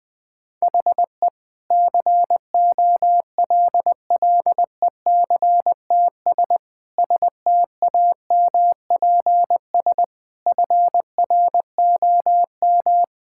Morse code, heard as a keyed tone, about 21 wpm, 700 Hz. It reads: HE COLLECTS STAMPS FROM
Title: Difficult Sentences with Hints - 20wpm